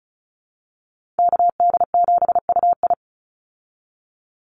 Morse code, heard as a keyed tone, about 34 wpm, 700 Hz. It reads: KD7UI
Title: Call Signs Repeated 2x - 35wpm